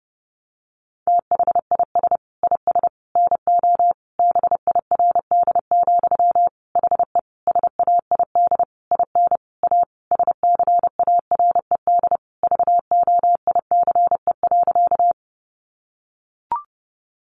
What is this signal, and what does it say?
Morse code, heard as a keyed tone, about 30 wpm, 700 Hz. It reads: THIS IS NO BIRD, HE SAID IN A SCARED VOICE.
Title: Sentences from Top 700 Words 30wpm